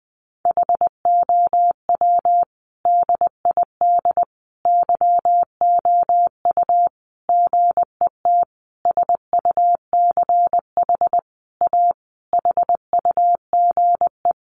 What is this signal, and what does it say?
Morse code, heard as a keyed tone, about 20 wpm, 700 Hz. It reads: HOW DID YOU GET SUCH A HUGE
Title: Sentences from Top 1100 Words 20wpm